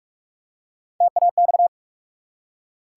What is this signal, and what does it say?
Morse code, heard as a keyed tone, about 44 wpm, 700 Hz. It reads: TAX
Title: Three Letter Words - 45wpm